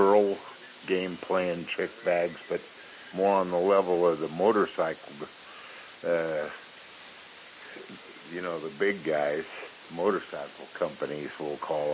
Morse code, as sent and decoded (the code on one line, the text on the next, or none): none